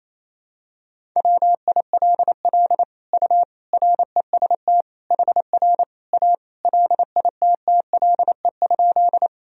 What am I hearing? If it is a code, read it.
Morse code, heard as a keyed tone, about 28 wpm, 700 Hz. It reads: WILL U REST HR A LITTLE?
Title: Sentences from Top 500 Words and Abbreviations 28wpm